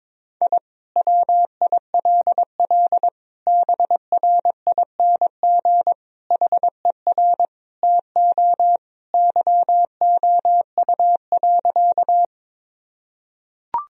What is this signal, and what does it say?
Morse code, heard as a keyed tone, about 22 wpm, 700 Hz. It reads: I WILL BRING HER TO YOU.
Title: Sentences from Top 1000 Words - Review Part 3 - 22wpm